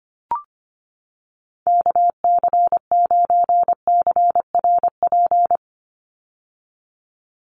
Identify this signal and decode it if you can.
Morse code, heard as a keyed tone, about 25 wpm, 690 Hz. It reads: KC9CRP
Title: USA Call Signs 25wpm